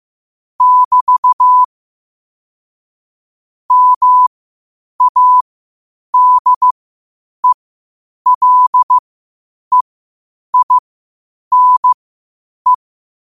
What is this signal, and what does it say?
Morse code, heard as a keyed tone, about 15 wpm, spaced as for 8 wpm, 1000 Hz. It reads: <BT> MADELEINE
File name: qotdfiles / 8wpm